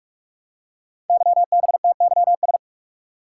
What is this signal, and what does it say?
Morse code, heard as a keyed tone, about 44 wpm, 700 Hz. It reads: YBTYS
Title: Code Groups - 5 Letters - 45wpm